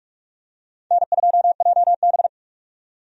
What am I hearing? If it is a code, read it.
Morse code, heard as a keyed tone, about 46 wpm, 700 Hz. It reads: N2JB
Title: USA Call Signs - 4 Characters - 45wpm